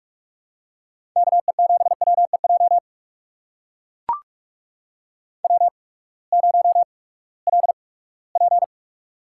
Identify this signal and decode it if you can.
Morse code, heard as a keyed tone, about 44 wpm, spaced as for 22 wpm, 700 Hz. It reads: KE7WEJ W0LP
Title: USA Call Signs - 8x Character Spacing - 45wpm